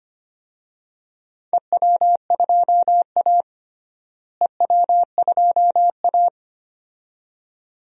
Morse code, heard as a keyed tone, about 25 wpm, 700 Hz. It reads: EW2A EW2A
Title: DXCC-100 Call Signs - 4 Characters - Repeated 2x - 25wpm